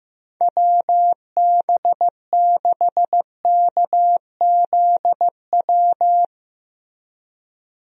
Morse code, heard as a keyed tone, about 16 wpm, 700 Hz. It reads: WB6KZW
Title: USA Call Signs - 6 Characters - Repeated 3x - 15wpm